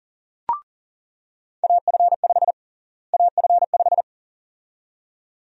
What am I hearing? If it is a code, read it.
Morse code, heard as a keyed tone, about 40 wpm, 700 Hz. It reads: AF5 AF5
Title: USA Call Sign Prefixes - Repeated 2x - 40wpm